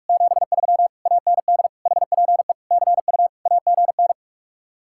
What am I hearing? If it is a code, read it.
Morse code, heard as a keyed tone, about 45 wpm, 700 Hz. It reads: 73 AND HPE CU AGN